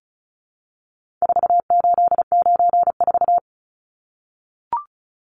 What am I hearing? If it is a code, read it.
Morse code, heard as a keyed tone, about 34 wpm, 700 Hz. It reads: 4894
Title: Numbers - 4-Digit 35wpm